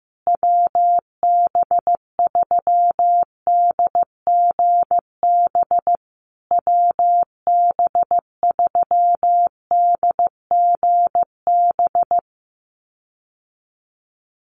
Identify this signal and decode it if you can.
Morse code, heard as a keyed tone, about 15 wpm, 700 Hz. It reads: WB3DGB WB3DGB
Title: USA Call Signs Repeated 3x 15wpm